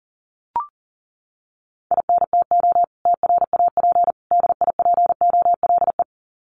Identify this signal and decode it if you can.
Morse code, heard as a keyed tone, about 40 wpm, 700 Hz. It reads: INTO TRAP DIPOLE
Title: Common QSO Phrases 40wpm